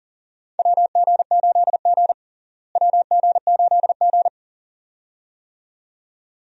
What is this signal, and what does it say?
Morse code, heard as a keyed tone, about 40 wpm, 700 Hz. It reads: WG8G WG8G
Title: USA Call Signs - 4 Characters - Repeated 2x - 40wpm